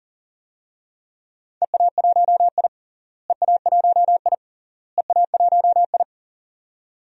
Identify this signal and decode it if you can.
Morse code, heard as a keyed tone, about 40 wpm, 700 Hz. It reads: EA1I EA1I EA1I
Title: DXCC-100 Call Signs - 4 Characters - Repeated 3x - 40wpm